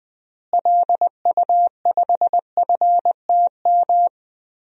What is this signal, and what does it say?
Morse code, heard as a keyed tone, about 21 wpm, 700 Hz. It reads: LU5FTM
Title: DXCC-100 Call Signs - 6 Characters - Repeated 2x - 20wpm